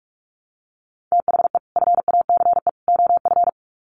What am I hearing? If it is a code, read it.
Morse code, heard as a keyed tone, about 44 wpm, 700 Hz. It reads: THE FACE OF
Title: Sets of 3 Words 45wpm